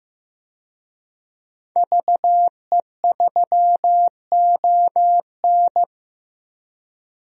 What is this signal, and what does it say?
Morse code, heard as a keyed tone, about 15 wpm, 700 Hz. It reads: VE3ON
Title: DXCC-100 Call Signs - 5 Characters - Repeated 2x - 15wpm